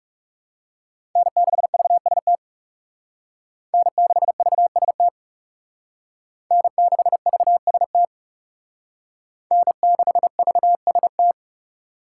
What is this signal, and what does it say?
Morse code, heard as a keyed tone, about 37 wpm, 700 Hz. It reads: N6VST N6VST N6VST N6VST
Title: Speed Racing - USA 1x3 Call Signs 30wpm